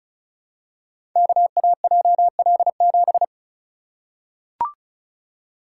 Morse code, heard as a keyed tone, about 36 wpm, 700 Hz. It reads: KAJL7
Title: Code Groups - 5 Letter:Numbers - 35wpm